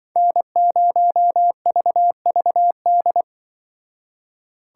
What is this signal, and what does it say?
Morse code, heard as a keyed tone, about 24 wpm, 700 Hz. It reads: N0VVD
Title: Speed Racing - USA 1x3 Call Signs 20wpm